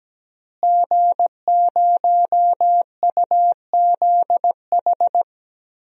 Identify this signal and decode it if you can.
Morse code, heard as a keyed tone, about 17 wpm, 700 Hz. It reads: G0UZH
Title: DXCC-100 Call Signs - 5 Characters - Repeated 2x - 17wpm